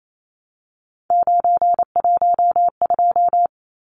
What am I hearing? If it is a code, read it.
Morse code, heard as a keyed tone, about 28 wpm, 700 Hz. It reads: 912
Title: Numbers - 3-Digit 28wpm